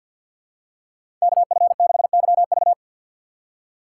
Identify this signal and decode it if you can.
Morse code, heard as a keyed tone, about 53 wpm, 700 Hz. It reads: KF6YV
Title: USA Call Signs - 5 Characters - 4x Character Spacing - 50wpm